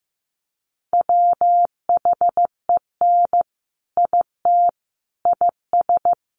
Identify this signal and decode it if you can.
Morse code, heard as a keyed tone, about 15 wpm, 700 Hz. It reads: WHEN IT IS
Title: Sentences from Top 1200 Words 15wpm